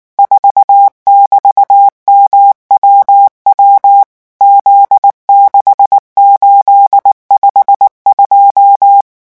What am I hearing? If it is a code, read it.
Morse code, heard as a keyed tone, about 19 wpm, 800 Hz. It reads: 4<BT>MWW Z6852
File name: morse mp3 / revision13-vitesse 18 mots